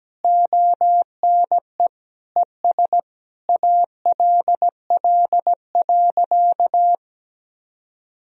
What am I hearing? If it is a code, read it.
Morse code, heard as a keyed tone, about 17 wpm, 700 Hz. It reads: ONE ES ALL.
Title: Sentences from Top 200 Words and Abbreviations 17wpm